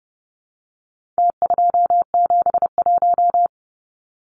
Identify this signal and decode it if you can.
Morse code, heard as a keyed tone, about 30 wpm, 700 Hz. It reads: T271
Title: Numbers - Contest Style - 4-Digit 30wpm